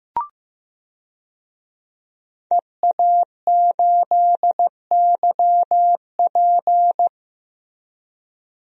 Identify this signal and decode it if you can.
Morse code, heard as a keyed tone, about 15 wpm, 700 Hz. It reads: EA8YP
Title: DXCC-100 Call Signs - 5 Characters - Repeated 3x - 15wpm